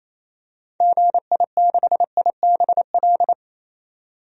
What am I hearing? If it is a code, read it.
Morse code, heard as a keyed tone, about 28 wpm, 700 Hz. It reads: GI6IBL
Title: DXCC-100 Call Signs - 6 Characters - 28wpm